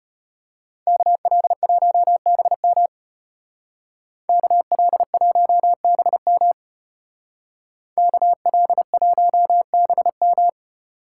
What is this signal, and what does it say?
Morse code, heard as a keyed tone, about 32 wpm, 700 Hz. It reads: KL1BM KL1BM KL1BM